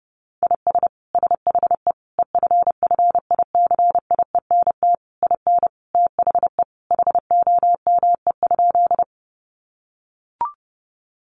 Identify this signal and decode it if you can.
Morse code, heard as a keyed tone, about 28 wpm, 700 Hz. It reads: IS SHE EFFICIENT IN THE HOME?
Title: Sentences from Top 1300 Words 30wpm